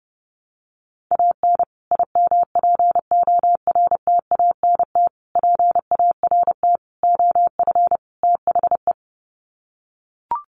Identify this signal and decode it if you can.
Morse code, heard as a keyed tone, about 30 wpm, 700 Hz. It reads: AN IMPORTANT PART OF THE
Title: Sets of 5 Words Encore 30wpm